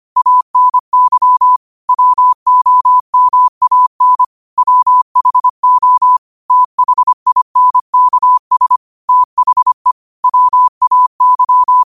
Morse code, heard as a keyed tone, about 25 wpm, 1000 Hz. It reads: ANY WOMAN WHO THINKS THE WAY
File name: qotdfiles / 25wpm